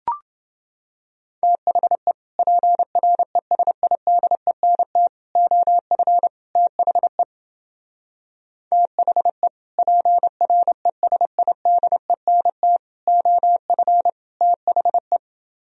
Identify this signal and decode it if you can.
Morse code, heard as a keyed tone, about 28 wpm, 700 Hz. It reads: THE PRESIDENT OF THE THE PRESIDENT OF THE